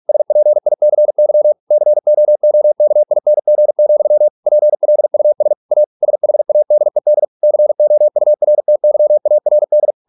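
Morse code, heard as a keyed tone, about 46 wpm, 600 Hz. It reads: SPICY COOKING, PLUS A SHADED COURTYARD
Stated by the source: Morse practice at 45 wpm, the novel Never by Ken Follett in English, from CW med Gnister.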